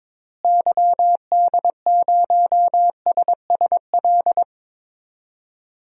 Morse code, heard as a keyed tone, about 22 wpm, 700 Hz. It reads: YD0SSL
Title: DXCC-100 Call Signs - 6 Characters - Repeated 2x - 22wpm